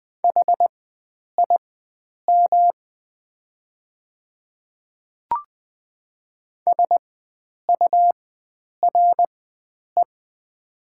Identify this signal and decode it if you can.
Morse code, heard as a keyed tone, about 20 wpm, spaced as for 9 wpm, 700 Hz. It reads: HIM SURE
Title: 4x Character Spacing - Top 300 Words - 20wpm